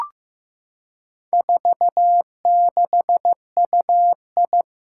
Morse code, heard as a keyed tone, about 16 wpm, 700 Hz. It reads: 46UI
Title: Four Challenging Letter-Number Combinations 15wpm